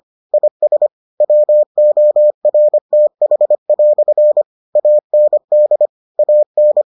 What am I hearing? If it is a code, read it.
Morse code, heard as a keyed tone, about 25 wpm, 600 Hz. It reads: IS WORTH" AND AN